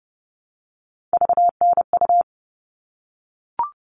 Morse code, heard as a keyed tone, about 30 wpm, 700 Hz. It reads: VNU
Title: All Cut Numbers - 3-Digit 30wpm